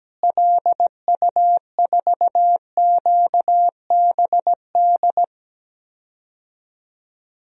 Morse code, heard as a keyed tone, about 17 wpm, 700 Hz. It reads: LU4QBD
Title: DXCC-100 Call Signs - 6 Characters - Repeated 3x - 17wpm